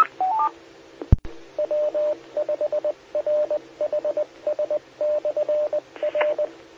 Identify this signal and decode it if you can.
Morse code, heard as a keyed tone, about 20 wpm, 600 Hz, with radio noise. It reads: W5RHS/R
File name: W5RHS SKYWARN Repeater System